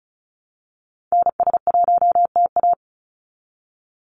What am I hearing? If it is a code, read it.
Morse code, heard as a keyed tone, about 36 wpm, 700 Hz. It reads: NS1TA
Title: NA SOTA Activator Callsigns 35wpm